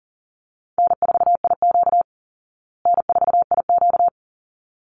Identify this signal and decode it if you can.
Morse code, heard as a keyed tone, about 40 wpm, 700 Hz. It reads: N4IQ N4IQ
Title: Easy Call Signs Repeated 2x - 40wpm